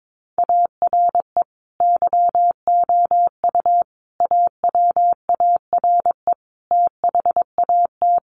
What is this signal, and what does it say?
Morse code, heard as a keyed tone, about 22 wpm, 700 Hz. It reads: ARE YOU AWARE THAT
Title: Sentences from Top 1200 Words 22wpm